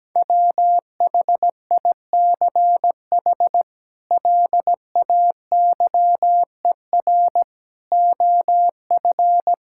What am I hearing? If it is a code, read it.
Morse code, heard as a keyed tone, about 17 wpm, 700 Hz. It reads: WHICH LAYER OF
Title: Sentences from Top 1600 Words 17wpm